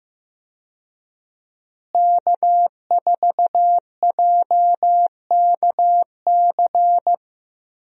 Morse code, heard as a keyed tone, about 15 wpm, 700 Hz. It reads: K4JKC
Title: USA Call Signs - 5 Characters - Repeated 3x - 15wpm